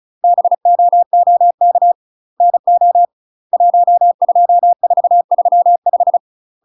Morse code, heard as a keyed tone, about 34 wpm, 700 Hz. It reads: BOOK NO 12435
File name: UK MASTER QSOs TRAINING / 35WPM / UK MASTER QSOs 0017